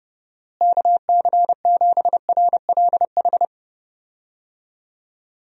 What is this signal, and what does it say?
Morse code, heard as a keyed tone, about 32 wpm, 700 Hz. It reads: KC7RLH
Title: USA Call Signs - 6 Characters - Repeated 3x - 30wpm